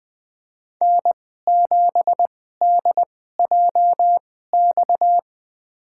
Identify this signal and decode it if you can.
Morse code, heard as a keyed tone, about 21 wpm, spaced as for 15 wpm, 700 Hz. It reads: N7DJX
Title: USA Call Signs - 5 Characters - 2x Character Spacing - 20wpm